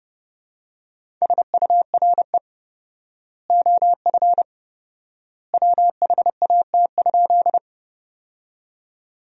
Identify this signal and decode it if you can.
Morse code, heard as a keyed tone, about 30 wpm, 700 Hz. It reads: SURE OF WHAT?
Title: 4x Word-Spacing - Sentences from Top 300 Words - 30wpm